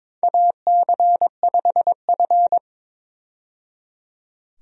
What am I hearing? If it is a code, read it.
Morse code, heard as a keyed tone, about 23 wpm, 700 Hz. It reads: AC5F
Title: USA Call Signs - 4 Characters - 22wpm